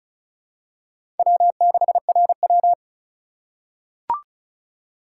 Morse code, heard as a keyed tone, about 36 wpm, 700 Hz. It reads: W6RW
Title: USA Call Signs - 4 Characters - 4x Character Spacing - 35wpm